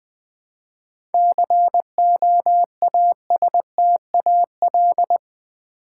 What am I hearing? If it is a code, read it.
Morse code, heard as a keyed tone, about 20 wpm, 700 Hz. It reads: COASTAL
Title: 7 Letter Words - First 3 Letters Given - 20wpm